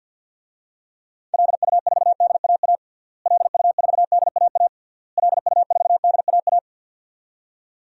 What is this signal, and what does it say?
Morse code, heard as a keyed tone, about 50 wpm, 700 Hz. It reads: LU4DAA LU4DAA LU4DAA
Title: DXCC-100 Call Signs - 6 Characters - Repeated 3x - 50wpm